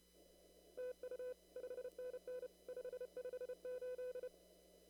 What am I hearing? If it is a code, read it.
Morse code, heard as a keyed tone, about 30 wpm, 500 Hz. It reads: TU 5NN 558
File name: CW / kt5j wpx cw-op-k5pi